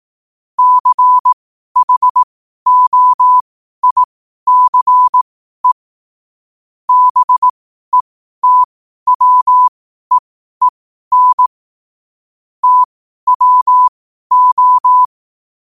Morse code, heard as a keyed tone, about 18 wpm, spaced as for 12 wpm, 1000 Hz. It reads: CHOICE BETWEEN TWO
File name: qotdfiles / 12wpm